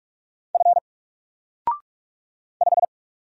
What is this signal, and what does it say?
Morse code, heard as a keyed tone, about 44 wpm, 700 Hz. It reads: F 5
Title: Single Letter-Number Alt 45wpm